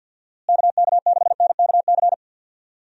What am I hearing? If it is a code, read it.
Morse code, heard as a keyed tone, about 52 wpm, 700 Hz. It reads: KK6NKC
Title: USA Call Signs - 6 Characters - Repeated 3x - 50wpm